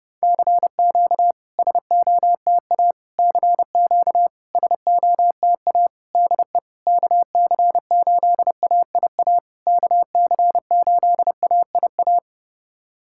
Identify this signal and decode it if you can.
Morse code, heard as a keyed tone, about 30 wpm, 700 Hz. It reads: CQ SOTA CQ SOTA DE KC8AIA KC8AIA
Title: SOTA QSO Practice 30wpm